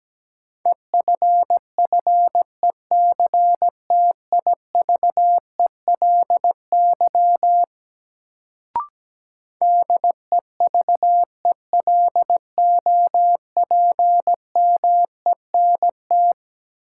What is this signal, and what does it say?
Morse code, heard as a keyed tone, about 17 wpm, 700 Hz. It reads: EFFECTIVELY DEVELOPMENT
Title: Sentences from Top 1400 Words - New Word Drill Alt - 17wpm